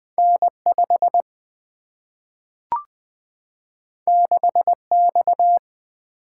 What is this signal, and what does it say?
Morse code, heard as a keyed tone, about 20 wpm, 700 Hz. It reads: N5 6X
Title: Two Letter-Number Combinations - 20wpm